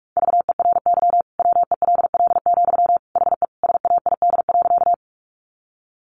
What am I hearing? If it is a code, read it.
Morse code, heard as a keyed tone, about 44 wpm, 700 Hz. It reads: VERY WELL, HE SAID.
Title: Sentences from Top 200 Words 45wpm